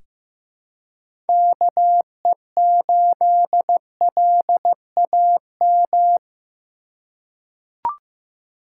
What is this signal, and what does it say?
Morse code, heard as a keyed tone, about 15 wpm, 700 Hz. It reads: KE8LAM
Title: USA Call Signs - 6 Characters - 4x Character Spacing - 15wpm